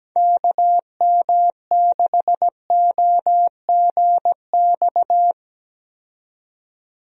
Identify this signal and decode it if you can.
Morse code, heard as a keyed tone, about 17 wpm, 700 Hz. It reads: KM6OGX
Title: USA Call Signs - 6 Characters - 17wpm